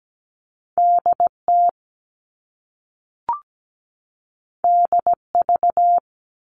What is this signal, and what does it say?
Morse code, heard as a keyed tone, about 17 wpm, 700 Hz. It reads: DT DV
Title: All Cut Numbers - 2-Digit 17wpm